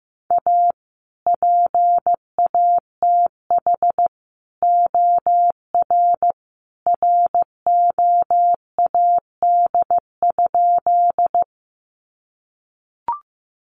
Morse code, heard as a keyed tone, about 15 wpm, 700 Hz. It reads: A PATH OR ROAD?
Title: Sentences from Top 1400 Words 15wpm